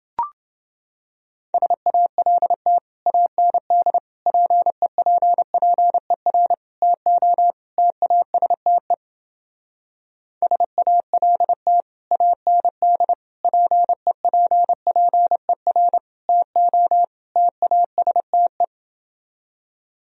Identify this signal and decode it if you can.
Morse code, heard as a keyed tone, about 28 wpm, 700 Hz. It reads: SALT AND PEPPER TO TASTE SALT AND PEPPER TO TASTE